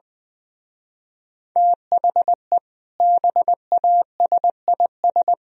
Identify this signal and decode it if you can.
Morse code, heard as a keyed tone, about 20 wpm, 700 Hz. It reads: THE BASIS